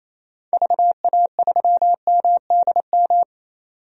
Morse code, heard as a keyed tone, about 28 wpm, 700 Hz. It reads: VA3MDM
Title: DXCC-100 Call Signs - 6 Characters - 28wpm